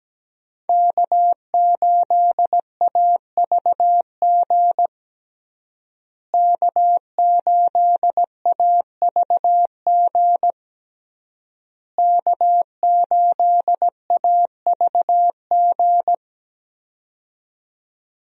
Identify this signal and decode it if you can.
Morse code, heard as a keyed tone, about 17 wpm, 700 Hz. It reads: K8AVG K8AVG K8AVG
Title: USA Call Signs - 5 Characters - Repeated 3x - 17wpm